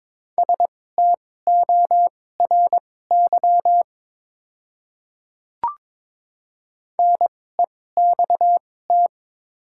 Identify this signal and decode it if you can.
Morse code, heard as a keyed tone, about 22 wpm, spaced as for 16 wpm, 700 Hz. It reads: STORY NEXT
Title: 2x Character Spacing - Top 300 Words - 22wpm